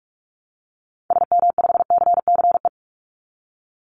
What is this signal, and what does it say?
Morse code, heard as a keyed tone, about 44 wpm, 700 Hz. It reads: SM5CCE
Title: Call Signs Repeated 3x - 45wpm